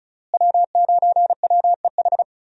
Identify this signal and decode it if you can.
Morse code, heard as a keyed tone, about 34 wpm, 700 Hz. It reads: W9WEH
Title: USA Call Signs - 5 Characters - Repeated 2x - 35wpm